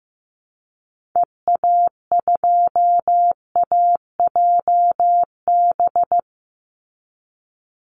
Morse code, heard as a keyed tone, about 15 wpm, 700 Hz. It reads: EA2AJB
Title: Call Signs Repeated 2x - 15wpm